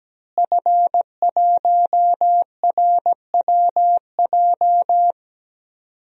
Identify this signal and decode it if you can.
Morse code, heard as a keyed tone, about 17 wpm, 700 Hz. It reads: F1RWJ
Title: DXCC-100 Call Signs - 5 Characters - Repeated 3x - 17wpm